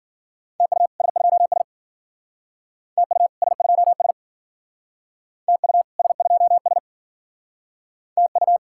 Morse code, heard as a keyed tone, about 52 wpm, 700 Hz. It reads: TU S2S TU S2S TU S2S TU